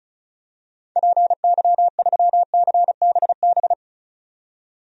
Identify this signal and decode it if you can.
Morse code, heard as a keyed tone, about 36 wpm, 700 Hz. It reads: PY3CBB
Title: DXCC-100 Call Signs - 6 Characters - 35wpm